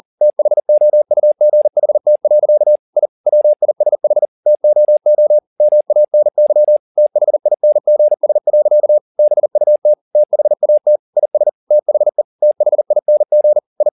Morse code, heard as a keyed tone, about 40 wpm, 600 Hz. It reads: THOUGHT. I WISH TOO MANY THINGS. BUT THAT IS THE THING I
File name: the old man and the sea 2 40wpm